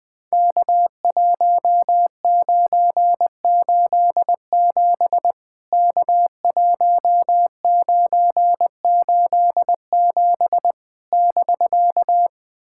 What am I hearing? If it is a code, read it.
Morse code, heard as a keyed tone, about 20 wpm, 700 Hz. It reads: K1987 K1987 <BK>
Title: POTA QSO Practice - 20wpm